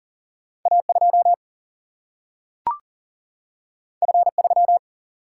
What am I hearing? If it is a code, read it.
Morse code, heard as a keyed tone, about 40 wpm, 700 Hz. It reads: A 2 F 3
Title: Single-Letter Single-Number 40wpm